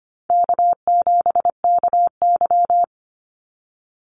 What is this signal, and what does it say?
Morse code, heard as a keyed tone, about 25 wpm, 700 Hz. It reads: K7KY
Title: Call Signs Repeated 3x - 25wpm